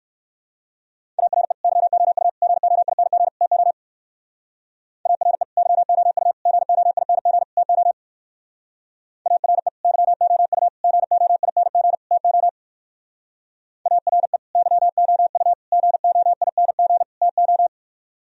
Morse code, heard as a keyed tone, about 56 wpm, 700 Hz. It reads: ARE YOU GOING TO ARE YOU GOING TO ARE YOU GOING TO ARE YOU GOING TO